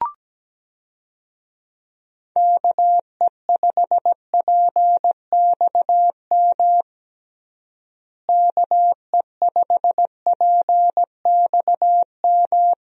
Morse code, heard as a keyed tone, about 17 wpm, 700 Hz. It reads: KE5PXM KE5PXM
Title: USA Call Signs - 6 Characters - Repeated 3x - 17wpm